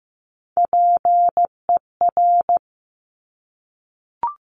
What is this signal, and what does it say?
Morse code, heard as a keyed tone, about 15 wpm, 700 Hz. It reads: PER
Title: Top Three Letter Combinations in Words 15wpm